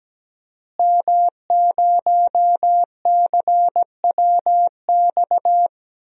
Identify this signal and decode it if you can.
Morse code, heard as a keyed tone, about 17 wpm, 700 Hz. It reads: M0CWX
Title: DXCC-100 Call Signs - 5 Characters - Repeated 3x - 17wpm